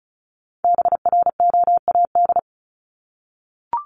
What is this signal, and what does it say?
Morse code, heard as a keyed tone, about 34 wpm, 700 Hz. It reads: BROAD
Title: Top 751-1000 Words 35wpm